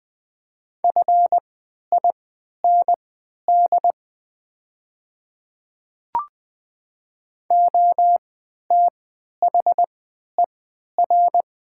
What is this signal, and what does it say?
Morse code, heard as a keyed tone, about 20 wpm, spaced as for 11 wpm, 690 Hz. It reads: FIND OTHER
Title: 3x Character Spacing - Top 100 Words - 20wpm